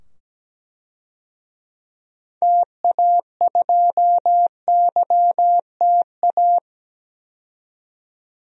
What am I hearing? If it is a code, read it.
Morse code, heard as a keyed tone, about 17 wpm, 700 Hz. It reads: TA2YTA
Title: DXCC-100 Call Signs - 6 Characters - 17wpm